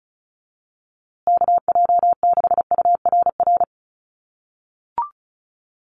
Morse code, heard as a keyed tone, about 34 wpm, 700 Hz. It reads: KJ6URR
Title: USA Call Signs Repeated 2x 35wpm